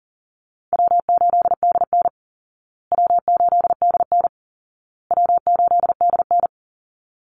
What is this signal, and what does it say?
Morse code, heard as a keyed tone, about 40 wpm, 700 Hz. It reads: W8DN W8DN W8DN
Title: Call Signs Repeated 2x - 40wpm